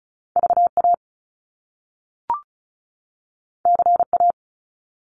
Morse code, heard as a keyed tone, about 36 wpm, 700 Hz. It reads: VA CA
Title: US States 35wpm